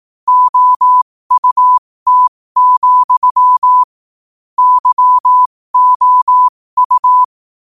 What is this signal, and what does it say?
Morse code, heard as a keyed tone, about 18 wpm, spaced as for 15 wpm, 1000 Hz. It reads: OUT, YOU
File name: qotdfiles / 15wpm